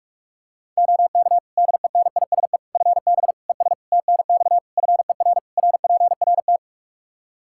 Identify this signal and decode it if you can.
Morse code, heard as a keyed tone, about 46 wpm, 700 Hz. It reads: OK DENISE FB ES TNX FER RPRT
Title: Practice QSO - Exchange 2 - Info - 45wpm